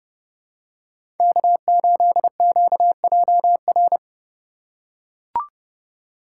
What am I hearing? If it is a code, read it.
Morse code, heard as a keyed tone, about 29 wpm, 700 Hz. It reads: K8QJR
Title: USA Call Signs - 5 Characters - 4x Character Spacing - 30wpm